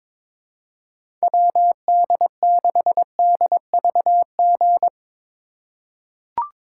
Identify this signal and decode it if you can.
Morse code, heard as a keyed tone, about 22 wpm, 700 Hz. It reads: WD6DVG
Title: USA Call Signs - 6 Characters - 3x Character Spacing - 22wpm